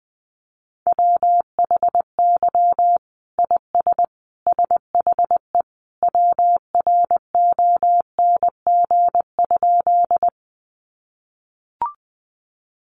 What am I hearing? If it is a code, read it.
Morse code, heard as a keyed tone, about 20 wpm, 700 Hz. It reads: WHY IS SHE WRONG?
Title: Sentences from Top 900 Words 20wpm